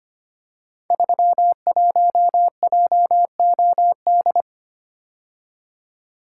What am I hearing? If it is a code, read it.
Morse code, heard as a keyed tone, about 25 wpm, 700 Hz. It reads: 31JOD
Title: Code Groups - 5 Letter:Numbers - 25wpm